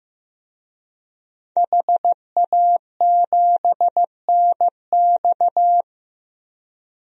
Morse code, heard as a keyed tone, about 15 wpm, 700 Hz. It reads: HA7NX
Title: DXCC-100 Call Signs - 5 Characters - 15wpm